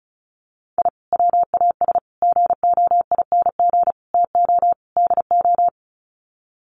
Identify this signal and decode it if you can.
Morse code, heard as a keyed tone, about 34 wpm, 700 Hz. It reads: I WAS GOING TO DO
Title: Sets of 5 Words Encore2 35wpm